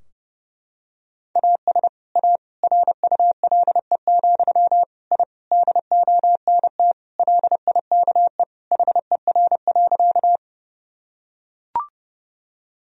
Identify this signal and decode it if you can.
Morse code, heard as a keyed tone, about 32 wpm, 700 Hz. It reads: AS A RULE, I DONT LIKE HER.
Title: Sentences from Top 1000 Words - Review Part 2 - 30wpm